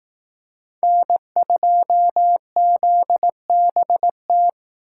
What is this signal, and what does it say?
Morse code, heard as a keyed tone, about 18 wpm, 700 Hz. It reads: N2ZBT
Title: Speed Racing - USA 1x3 Call Signs 15wpm